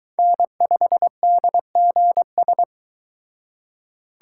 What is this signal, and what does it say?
Morse code, heard as a keyed tone, about 22 wpm, 700 Hz. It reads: N5DGS